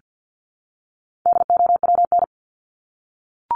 Morse code, heard as a keyed tone, about 50 wpm, 700 Hz. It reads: DOWN
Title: Top 300 Words Repeated 50wpm